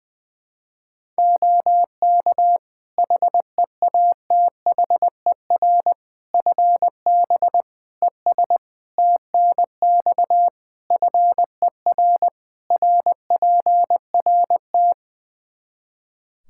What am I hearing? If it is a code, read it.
Morse code, heard as a keyed tone, about 20 wpm, 700 Hz. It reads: OK HEATHER FB ES TNX FER RPRT
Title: K1USN SST QSO Practice 20wpm